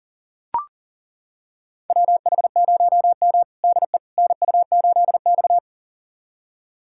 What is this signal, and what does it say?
Morse code, heard as a keyed tone, about 40 wpm, 700 Hz. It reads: WH0M DE NU8X
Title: USA Callsign DE Callsign - 4 Characters - 40wpm